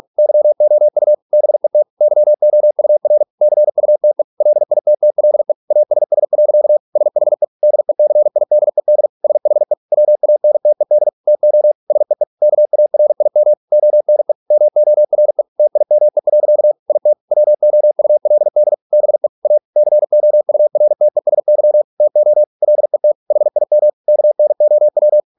Morse code, heard as a keyed tone, about 46 wpm, 600 Hz. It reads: YOU BET YOUR CUTE LITTLE ASS. SHE DECIDED SHE WANTED TO SEE KARIM ONE MORE TIME. IT WOULD BE A COURTESY TO LET HIM KNOW